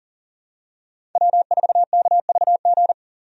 Morse code, heard as a keyed tone, about 40 wpm, 700 Hz. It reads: W4KVG
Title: USA Call Signs - 5 Characters - 40wpm